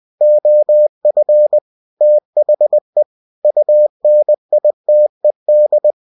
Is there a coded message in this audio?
Yes: Morse code from 0.2 to 5.9 s, heard as a keyed tone, about 20 wpm, 600 Hz. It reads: OF THE UNITED